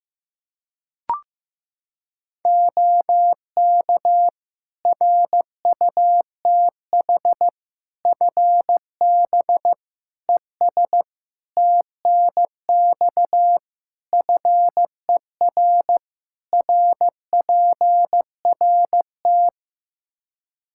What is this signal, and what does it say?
Morse code, heard as a keyed tone, about 15 wpm, 700 Hz. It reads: OK RUTH FB ES TNX FER RPRT
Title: Practice QSO - Exchange 2 - Info - 15wpm